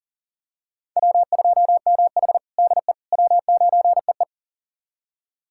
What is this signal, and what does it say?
Morse code, heard as a keyed tone, about 40 wpm, 700 Hz. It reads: W2MH DE W9EE
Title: USA Callsign DE Callsign - 4 Characters - 40wpm